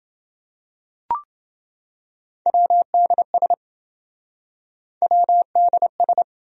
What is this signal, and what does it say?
Morse code, heard as a keyed tone, about 27 wpm, 700 Hz. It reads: WDS WDS